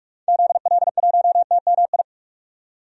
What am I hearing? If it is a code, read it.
Morse code, heard as a keyed tone, about 46 wpm, 700 Hz. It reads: ZL1TMI
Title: DXCC-100 Call Signs - 6 Characters - 45wpm